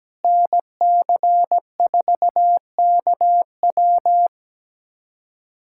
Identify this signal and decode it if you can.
Morse code, heard as a keyed tone, about 17 wpm, 700 Hz. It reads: NC4KW
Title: USA Callsign DE Callsign - 5 Characters - 17wpm